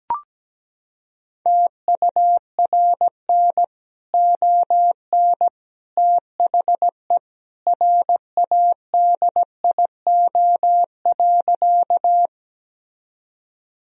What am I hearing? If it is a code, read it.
Morse code, heard as a keyed tone, about 17 wpm, 700 Hz. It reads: TURN ON THE RADIO.
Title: Sentences from Top 1000 Words - Review Part 3 - 17wpm